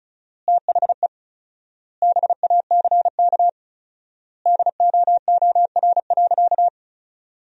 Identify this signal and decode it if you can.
Morse code, heard as a keyed tone, about 34 wpm, 700 Hz. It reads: THE BACK DOOR.
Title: 4x Word-Spacing - Sentences from Top 300 Words - 35wpm